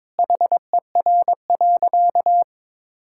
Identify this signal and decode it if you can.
Morse code, heard as a keyed tone, about 22 wpm, 700 Hz. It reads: HER.
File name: practice / 22wpm / Sentences with 4-Word Phrases - 22wpm